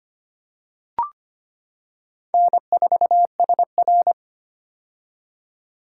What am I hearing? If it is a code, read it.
Morse code, heard as a keyed tone, about 25 wpm, 700 Hz. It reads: N4SR
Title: K1USN SST QSO Practice - 25wpm